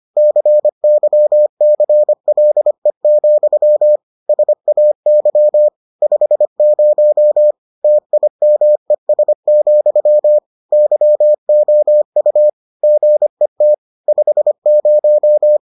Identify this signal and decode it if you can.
Morse code, heard as a keyed tone, about 25 wpm, 600 Hz. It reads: CYCLE, SAY 50 TIMES, YOU GET 50